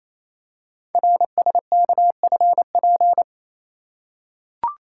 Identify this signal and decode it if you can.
Morse code, heard as a keyed tone, about 28 wpm, 700 Hz. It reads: RSKFP
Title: Code Groups - 5 Letters - 28wpm